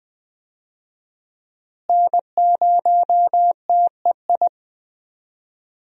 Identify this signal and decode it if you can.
Morse code, heard as a keyed tone, about 20 wpm, 700 Hz. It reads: N0TEI
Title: USA Call Signs - 5 Characters - Repeated 3x - 20wpm